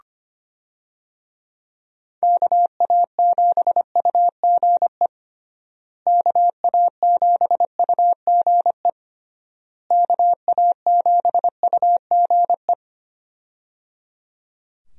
Morse code, heard as a keyed tone, about 25 wpm, 700 Hz. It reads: KA7UGE KA7UGE KA7UGE
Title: USA Call Signs - 6 Characters - Repeated 3x - 25wpm